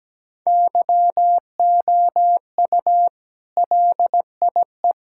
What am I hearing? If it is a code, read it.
Morse code, heard as a keyed tone, about 17 wpm, 700 Hz. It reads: YOU LIE
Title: Sentences from Top 1600 Words 17wpm